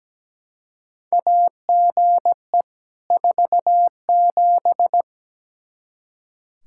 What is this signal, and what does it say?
Morse code, heard as a keyed tone, about 17 wpm, 700 Hz. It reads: AGE 47
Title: K1USN SST QSO Practice 17wpm